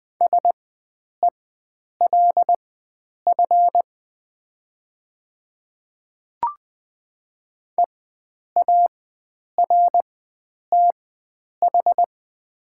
Morse code, heard as a keyed tone, about 20 wpm, spaced as for 9 wpm, 700 Hz. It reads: SELF EARTH
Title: 4x Character Spacing - Top 300 Words - 20wpm